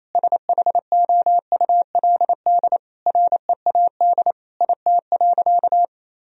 Morse code, heard as a keyed tone, about 28 wpm, 700 Hz. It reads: SHOULD READ IT.
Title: Sentences from Top 1000 Words - Review Part 1 - 28wpm